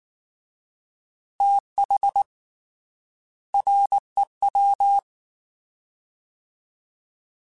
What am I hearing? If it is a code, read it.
Morse code, heard as a keyed tone, about 19 wpm, 800 Hz. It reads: TH REW
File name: Fast TH @18